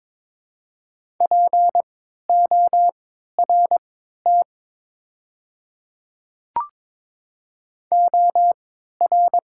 Morse code, heard as a keyed tone, about 22 wpm, spaced as for 12 wpm, 700 Hz. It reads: PORT OR
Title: 3x Character Spacing - Top 200 Words - 22wpm